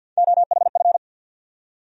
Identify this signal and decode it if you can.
Morse code, heard as a keyed tone, about 53 wpm, 700 Hz. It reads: OHF
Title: USA Call Sign Postfixes - 3 Letters 50wpm